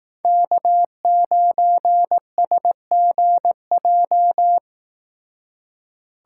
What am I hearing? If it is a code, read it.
Morse code, heard as a keyed tone, about 18 wpm, 700 Hz. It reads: K9SGJ
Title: Speed Racing - USA 1x3 Call Signs 15wpm